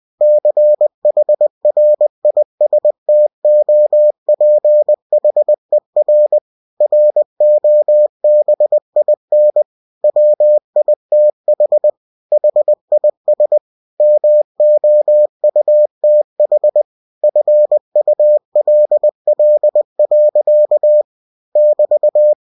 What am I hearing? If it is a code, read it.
Morse code, heard as a keyed tone, about 20 wpm, 600 Hz. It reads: CHRISTOPHER ROBIN WITH HIS MOUTH FULL. <BT>
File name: cw-books / winnie-the-pooh / winnie-20wpm-page-0083